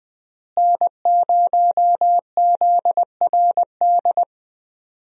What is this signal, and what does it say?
Morse code, heard as a keyed tone, about 21 wpm, 700 Hz. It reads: N0ZRD
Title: USA Call Signs - 5 Characters - Repeated 2x - 20wpm